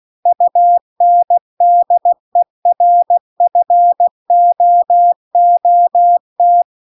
Morse code, heard as a keyed tone, about 16 wpm, 700 Hz. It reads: UNDERFOOT